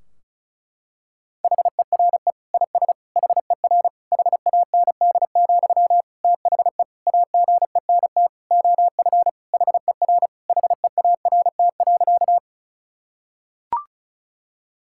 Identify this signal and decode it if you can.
Morse code, heard as a keyed tone, about 34 wpm, 700 Hz. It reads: HERE IS HER HAND, THE AGENT OF HER HEART.
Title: Sentences from Top 1600 Words 35wpm